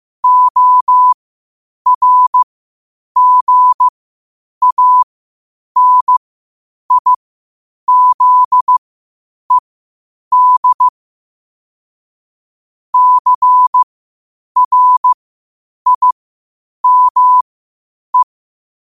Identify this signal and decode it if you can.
Morse code, heard as a keyed tone, about 15 wpm, spaced as for 8 wpm, 1000 Hz. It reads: ORGANIZED CRIME
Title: 8 WPM morse code quotes for Fri, 01 Mar 2024